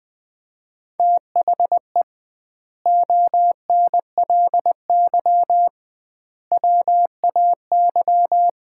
Morse code, heard as a keyed tone, about 20 wpm, 700 Hz. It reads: THE ONLY WAY
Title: 2x Word-Spacing - Sets of 4 Words - 20wpm